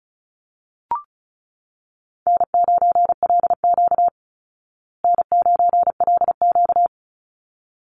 Morse code, heard as a keyed tone, about 34 wpm, 700 Hz. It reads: N9LQ N9LQ
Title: Easy Call Signs Repeated 2x - 35wpm